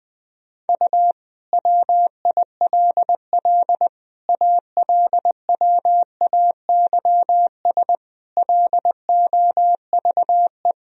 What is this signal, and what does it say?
Morse code, heard as a keyed tone, about 20 wpm, 700 Hz. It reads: U WILL ALWAYS LOVE
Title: Sentences from Top 400 Words and Abbreviations 20wpm